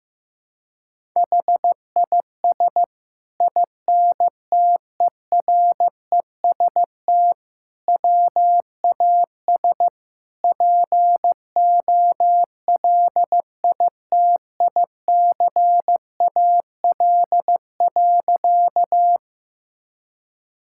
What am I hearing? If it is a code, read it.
Morse code, heard as a keyed tone, about 15 wpm, 700 Hz. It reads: HIS INTEREST WAS POLITICAL.
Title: Sentences from Top 1000 Words - Review Part 5 - 15wpm